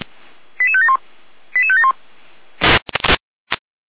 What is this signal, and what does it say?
Five-tone selective call (ZVEI), heard as quick runs of five tones, tones of about 70 ms. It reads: 88571 88571